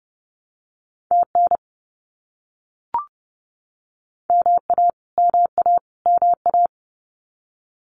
Morse code, heard as a keyed tone, about 30 wpm, 700 Hz. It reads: TN MA MA MA
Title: US States Repeated 3x 30wpm